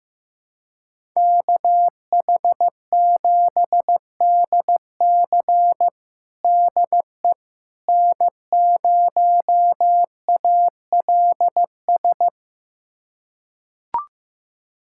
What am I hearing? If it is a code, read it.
Morse code, heard as a keyed tone, about 16 wpm, 700 Hz. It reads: KH7DC DE N0ALS
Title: USA Callsign DE Callsign - 5 Characters - 15wpm